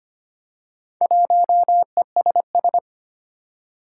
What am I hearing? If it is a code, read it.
Morse code, heard as a keyed tone, about 25 wpm, 700 Hz. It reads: 1ESS
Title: Four Challenging Letter-Number Combinations 25wpm